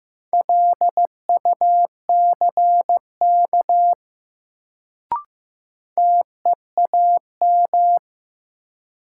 Morse code, heard as a keyed tone, about 15 wpm, 700 Hz. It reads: LUCK TEAM
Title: Sentences from Top 1500 Words - New Word Drill Alt - 15wpm